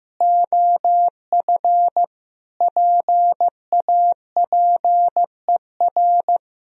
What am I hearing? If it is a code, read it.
Morse code, heard as a keyed tone, about 16 wpm, 700 Hz. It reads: OF PAPER
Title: Sentences from Top 1000 Words - Review Part 2 - 15wpm